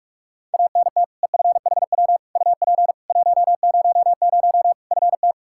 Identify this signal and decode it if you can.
Morse code, heard as a keyed tone, about 41 wpm, 700 Hz. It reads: ANT EFHW UP 100 FT
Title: Practice QSO - Exchange 2 - Info - 45wpm